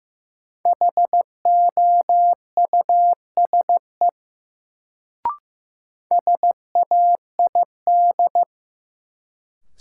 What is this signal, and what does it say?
Morse code, heard as a keyed tone, about 15 wpm, 700 Hz. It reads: HOUSE SAID
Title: Sets of 2 Words - New Word Drill - 15wpm